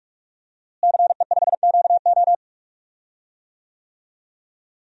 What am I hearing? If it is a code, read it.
Morse code, heard as a keyed tone, about 43 wpm, 700 Hz. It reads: CE5QO
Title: DXCC-100 Call Signs - 5 Characters - Repeated 2x - 45wpm